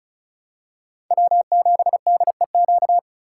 Morse code, heard as a keyed tone, about 36 wpm, 700 Hz. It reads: W7DEQ
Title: USA Call Signs - 5 Characters - 4x Character Spacing - 35wpm